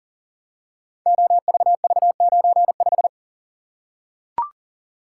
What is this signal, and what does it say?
Morse code, heard as a keyed tone, about 41 wpm, 700 Hz. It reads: OVV95
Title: Code Groups - 5 Letter:Numbers - 40wpm